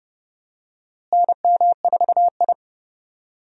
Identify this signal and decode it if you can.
Morse code, heard as a keyed tone, about 31 wpm, 700 Hz. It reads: NM4I
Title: USA Call Signs - 4 Characters - Repeated 3x - 30wpm